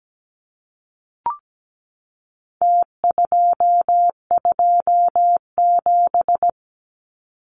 Morse code, heard as a keyed tone, about 17 wpm, 700 Hz. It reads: T227
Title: Numbers - Contest Style - 4-Digit 17wpm